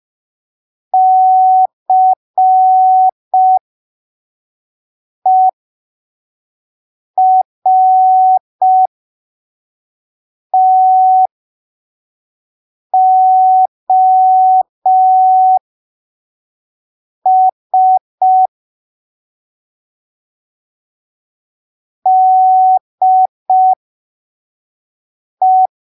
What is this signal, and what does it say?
Morse code, heard as a keyed tone, about 5 wpm, spaced as for 3 wpm, 750 Hz. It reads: CERTOS DE
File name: arquivos / texto / texto 15 05ppm